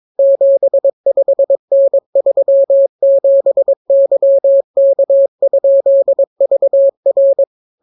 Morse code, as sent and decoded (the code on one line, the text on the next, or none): --... ..... -. ...-- --... -.-- -.- ..--.. ...- .-.
75N37YK?VR